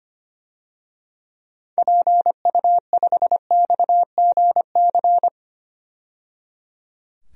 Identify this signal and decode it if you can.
Morse code, heard as a keyed tone, about 25 wpm, 700 Hz. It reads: PU5XGC
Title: DXCC-100 Call Signs - 6 Characters - 25wpm